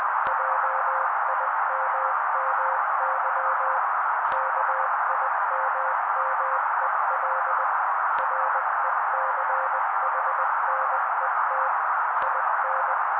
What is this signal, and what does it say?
Morse code, heard as a keyed tone, about 20 wpm, 530 Hz. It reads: JIMMY KIMMEL RECHNET IN
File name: daily / 18wpm noise